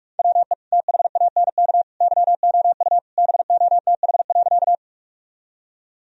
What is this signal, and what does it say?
Morse code, heard as a keyed tone, about 46 wpm, 700 Hz. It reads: WE THANK YOU BOTH.
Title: Sentences from Top 1000 Words - Review Part 1 - 45wpm